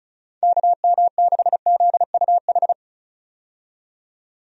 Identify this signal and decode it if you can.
Morse code, heard as a keyed tone, about 36 wpm, 700 Hz. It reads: KM6ZUH
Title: USA Call Signs - 6 Characters - 35wpm